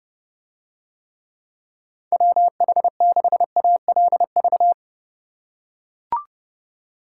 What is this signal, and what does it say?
Morse code, heard as a keyed tone, about 29 wpm, 700 Hz. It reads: WH6ALV